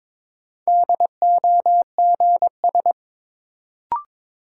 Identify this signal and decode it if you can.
Morse code, heard as a keyed tone, about 22 wpm, 700 Hz. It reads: DOGS
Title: Sentences from Top 1600 Words - New Word Drill - 22wpm